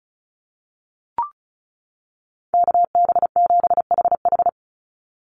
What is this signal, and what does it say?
Morse code, heard as a keyed tone, about 34 wpm, 700 Hz. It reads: KB7HH
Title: NA SOTA Chaser Callsigns 35wpm